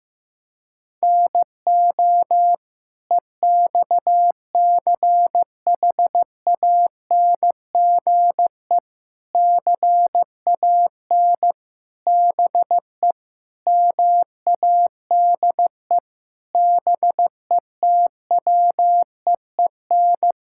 Morse code, heard as a keyed tone, about 15 wpm, 700 Hz. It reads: NO EXCHANGE CAN BE MADE BETWEEN
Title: Sentences from Top 1000 Words - Review Part 6 - 15wpm